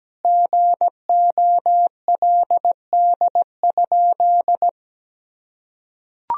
Morse code, heard as a keyed tone, about 17 wpm, 700 Hz. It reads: GOLD?
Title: Sentences from Top 1000 Words - Review Part 3 - 17wpm